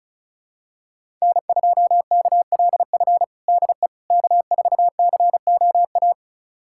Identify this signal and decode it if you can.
Morse code, heard as a keyed tone, about 34 wpm, 700 Hz. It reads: N2KLF DE K4COA
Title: USA Callsign DE Callsign - 5 Characters - 35wpm